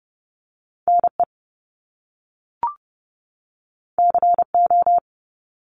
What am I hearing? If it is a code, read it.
Morse code, heard as a keyed tone, about 30 wpm, 700 Hz. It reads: NE CO
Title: Top Two Letter Combinations in Words 30wpm